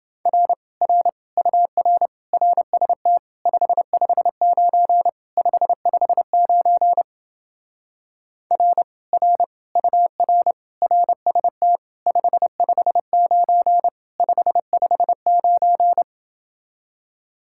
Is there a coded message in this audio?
Yes: Morse code from 0.3 to 16.0 s, heard as a keyed tone, about 29 wpm, 700 Hz. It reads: R R UR RST 559 559 R R UR RST 559 559